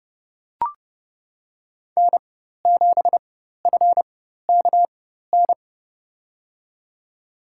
Morse code, heard as a keyed tone, about 30 wpm, spaced as for 14 wpm, 700 Hz. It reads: N7FKN
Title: USA Call Signs - 5 Characters - 4x Character Spacing - 30wpm